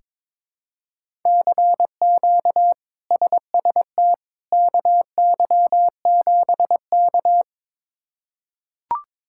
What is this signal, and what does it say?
Morse code, heard as a keyed tone, about 22 wpm, 700 Hz. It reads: CQ SST KY7K
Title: K1USN SST QSO Practice - 22wpm